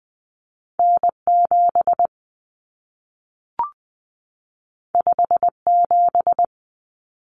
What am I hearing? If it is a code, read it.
Morse code, heard as a keyed tone, about 20 wpm, 700 Hz. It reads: N7 57
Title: Common Cut Numbers - 2-Digit 20wpm